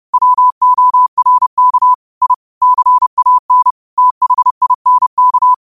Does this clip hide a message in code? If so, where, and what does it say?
Yes: Morse code from 0.1 to 5.5 s, heard as a keyed tone, about 30 wpm, 1000 Hz. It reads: WORK I CAN THINK